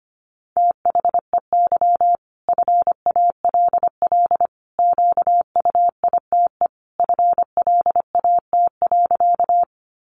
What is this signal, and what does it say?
Morse code, heard as a keyed tone, about 25 wpm, 700 Hz. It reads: THEY FALL QUITE FLAT.
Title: Sentences from Top 1000 Words 25wpm